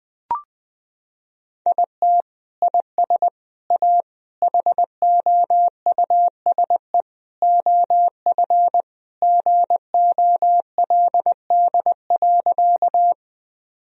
Morse code, heard as a keyed tone, about 20 wpm, 700 Hz. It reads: IT IS A HOUSE OF GOLD.
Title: Sentences from Top 1000 Words - Review Part 3 - 20wpm